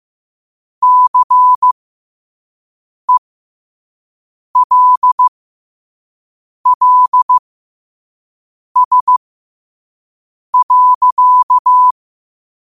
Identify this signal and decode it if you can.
Morse code, heard as a keyed tone, about 15 wpm, spaced as for 5 wpm, 1000 Hz. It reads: CELLS.
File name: 2024-10-07-5wpm